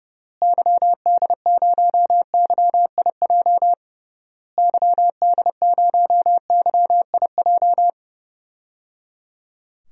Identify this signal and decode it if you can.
Morse code, heard as a keyed tone, about 30 wpm, 700 Hz. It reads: YD0YIJ YD0YIJ
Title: DXCC-100 Call Signs - 6 Characters - Repeated 3x - 30wpm